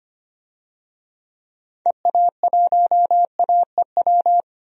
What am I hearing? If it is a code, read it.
Morse code, heard as a keyed tone, about 25 wpm, 700 Hz. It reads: EA1AEW
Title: DXCC-100 Call Signs - 6 Characters - Repeated 3x - 25wpm